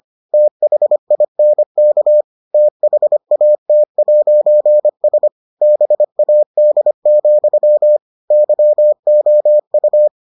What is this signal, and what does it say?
Morse code, heard as a keyed tone, about 25 wpm, 600 Hz. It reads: THINK THAT'S BAD, YOU